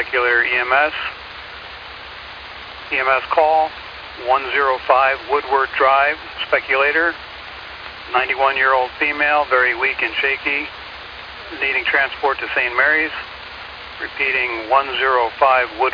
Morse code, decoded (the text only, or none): none